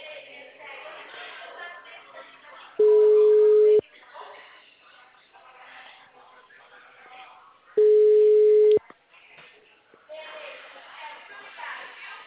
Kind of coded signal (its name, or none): telephone call progress tone